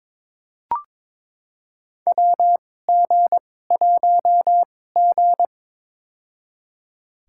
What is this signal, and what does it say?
Morse code, heard as a keyed tone, about 22 wpm, spaced as for 16 wpm, 700 Hz. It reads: WG1G
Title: USA Call Signs - 2x Character Spacing - 22wpm